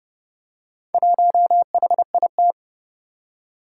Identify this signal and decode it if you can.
Morse code, heard as a keyed tone, about 29 wpm, 700 Hz. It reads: 1HIT
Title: Four Challenging Letter-Number Combinations 30wpm